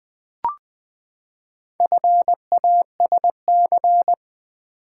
Morse code, heard as a keyed tone, about 20 wpm, 700 Hz. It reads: FASC
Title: Build A Word - 4 Letters At A Time - 11-Letter Words - 20wpm